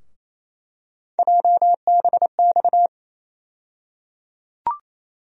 Morse code, heard as a keyed tone, about 27 wpm, 700 Hz. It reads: JBX
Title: USA Call Sign Postfixes - 3 Letters 28wpm